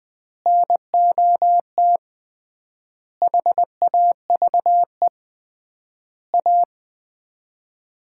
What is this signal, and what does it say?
Morse code, heard as a keyed tone, about 21 wpm, 700 Hz. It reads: NOT HAVE A
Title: 3x Word-Spacing - Sets of 5 Words - 20wpm